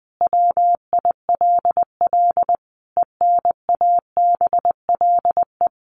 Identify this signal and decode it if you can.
Morse code, heard as a keyed tone, about 20 wpm, 700 Hz. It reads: WILL ENABLE
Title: Sentences from Top 1500 Words 20wpm